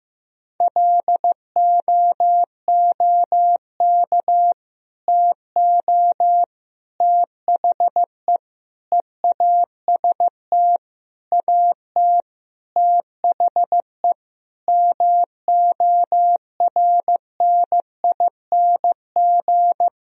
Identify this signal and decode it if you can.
Morse code, heard as a keyed tone, about 15 wpm, 700 Hz. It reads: LOOK TO THE EAST AT THE MORNING
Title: Sentences from Top 1000 Words - Review Part 3 - 15wpm